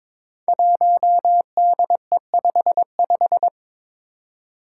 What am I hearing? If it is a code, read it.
Morse code, heard as a keyed tone, about 22 wpm, 700 Hz. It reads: 1DE55
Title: Five Challenging Letter-Number Combinations 22wpm